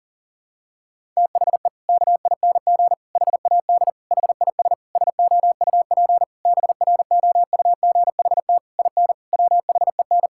Morse code, heard as a keyed tone, about 40 wpm, 700 Hz. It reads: THE KING HAD HIS SOUP BROUGHT IN WHEN
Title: Sentences from Top 1000 Words - Review Part 6 - 40wpm